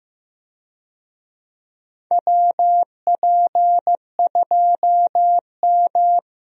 Morse code, heard as a keyed tone, about 15 wpm, 700 Hz. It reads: WP2M
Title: USA Call Signs - 4 Characters - Repeated 3x - 15wpm